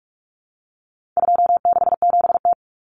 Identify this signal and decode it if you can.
Morse code, heard as a keyed tone, about 44 wpm, 700 Hz. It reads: 267T
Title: Common Cut Numbers - 4-Digit 45wpm